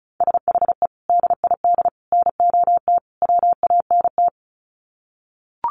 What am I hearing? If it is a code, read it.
Morse code, heard as a keyed tone, about 34 wpm, 700 Hz. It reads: SHE DID NOT WANT
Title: Sets of 4 Words 35wpm